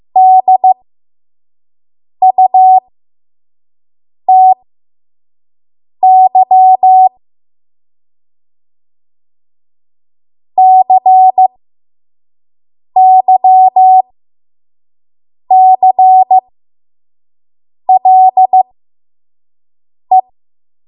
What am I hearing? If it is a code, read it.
Morse code, heard as a keyed tone, about 15 wpm, spaced as for 5 wpm, 750 Hz. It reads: DUTY CYCLE
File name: Morse / 05WPM